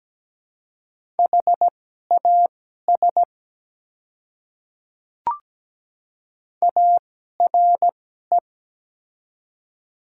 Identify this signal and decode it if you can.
Morse code, heard as a keyed tone, about 17 wpm, spaced as for 12 wpm, 700 Hz. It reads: HAS ARE
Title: 2x Character Spacing - Top 100 Words - 17wpm